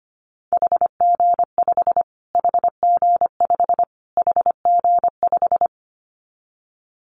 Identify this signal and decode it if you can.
Morse code, heard as a keyed tone, about 25 wpm, 700 Hz. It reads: HG5 HG5 HG5
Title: Callsign Prefixes for Top 25 DXCC - Repeated 3x - 25wpm